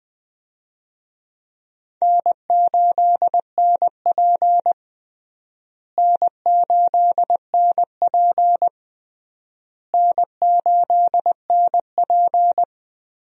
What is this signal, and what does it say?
Morse code, heard as a keyed tone, about 20 wpm, 700 Hz. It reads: N8NP N8NP N8NP
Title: USA Call Signs - 4 Characters - Repeated 3x - 20wpm